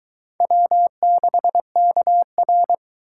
Morse code, heard as a keyed tone, about 22 wpm, 700 Hz. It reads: W6KR